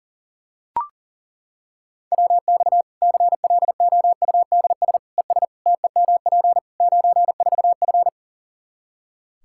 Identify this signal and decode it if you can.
Morse code, heard as a keyed tone, about 40 wpm, 700 Hz. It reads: WX CLOUDS ES TEMP 94F
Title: Practice QSO - Exchange 2 - Info - 40wpm